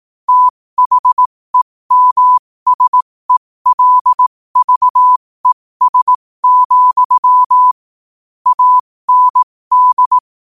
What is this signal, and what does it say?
Morse code, heard as a keyed tone, about 18 wpm, spaced as for 15 wpm, 1000 Hz. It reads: THEMSELVES, AND
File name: qotdfiles / 15wpm